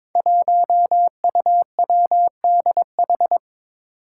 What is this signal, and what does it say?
Morse code, heard as a keyed tone, about 23 wpm, 700 Hz. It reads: 1UWDH
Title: Five Challenging Letter-Number Combinations 22wpm